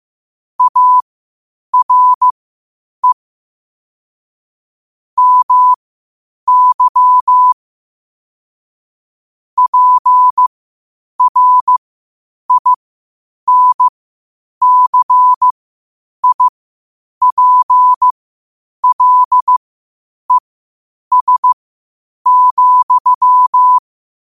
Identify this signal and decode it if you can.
Morse code, heard as a keyed tone, about 15 wpm, spaced as for 8 wpm, 1000 Hz. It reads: ARE MY PRINCIPLES,
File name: qotdfiles / 8wpm